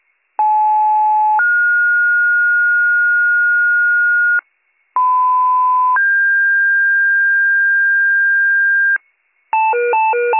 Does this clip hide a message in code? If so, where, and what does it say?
Yes: two-tone page from 0.4 to 9.0 s, heard as a one-second tone, then a three-second tone, about 860 then 1430 Hz, about 1010 then 1600 Hz, with noise.